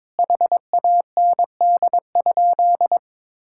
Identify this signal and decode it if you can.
Morse code, heard as a keyed tone, about 22 wpm, 700 Hz. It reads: HAND?
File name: Sentences with 9-Word Phrases - 22wpm